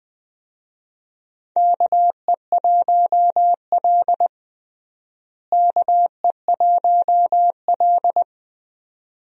Morse code, heard as a keyed tone, about 21 wpm, 700 Hz. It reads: KE1L KE1L
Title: USA Call Signs - 4 Characters - Repeated 2x - 20wpm